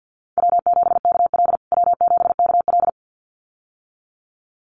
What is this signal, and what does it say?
Morse code, heard as a keyed tone, about 49 wpm, 700 Hz. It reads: W7KL W7KL
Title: USA Call Signs Repeated 2x 50wpm